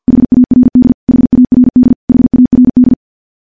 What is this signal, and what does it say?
Morse code, heard as a keyed tone, about 50 wpm, 250 Hz. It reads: HARD HARD HARD
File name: morse / morse-training-50WPM